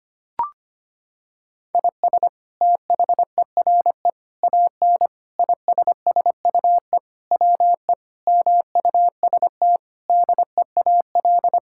Morse code, heard as a keyed tone, about 25 wpm, 700 Hz. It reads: IS THERE AN ISSUE WE MUST DEAL
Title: Sentences from Top 1000 Words - Review Part 6 - 25wpm